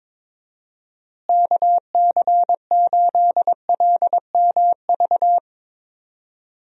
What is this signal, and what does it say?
Morse code, heard as a keyed tone, about 22 wpm, 700 Hz. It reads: KC8LMV
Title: USA Call Signs - 6 Characters - Repeated 2x - 22wpm